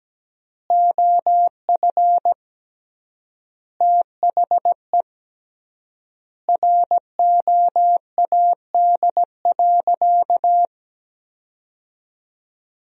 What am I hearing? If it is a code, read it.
Morse code, heard as a keyed tone, about 17 wpm, 700 Hz. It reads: OF THE ROAD.
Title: 3x Word-Spacing - Sentences from Top 400 Words - 17wpm